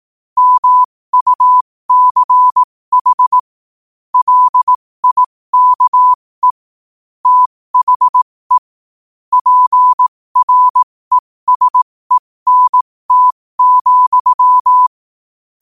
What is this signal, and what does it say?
Morse code, heard as a keyed tone, about 18 wpm, spaced as for 15 wpm, 1000 Hz. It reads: MUCH LIKE THE PRESENT,
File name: qotdfiles / 15wpm